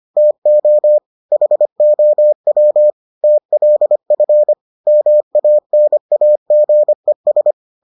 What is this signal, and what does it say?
Morse code, heard as a keyed tone, about 25 wpm, 600 Hz. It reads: TO HOW TLF MANAGES